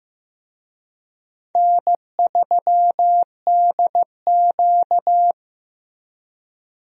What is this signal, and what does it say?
Morse code, heard as a keyed tone, about 15 wpm, 700 Hz. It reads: N3DQ
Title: USA Call Signs - 4 Characters - Repeated 2x - 15wpm